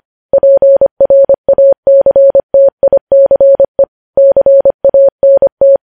Morse code, heard as a keyed tone, about 25 wpm, 550 Hz. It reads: PRACTICE CANT
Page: news headlines in morse code at 25 wpm